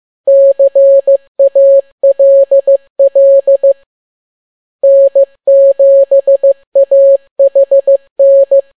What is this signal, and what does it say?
Morse code, heard as a keyed tone, about 15 wpm, 550 Hz. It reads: CALL N7AHN